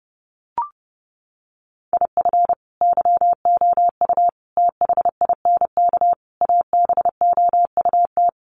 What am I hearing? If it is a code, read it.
Morse code, heard as a keyed tone, about 30 wpm, 700 Hz. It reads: IF YOU THINK ABOUT
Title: Sets of 5 Words Encore2 30wpm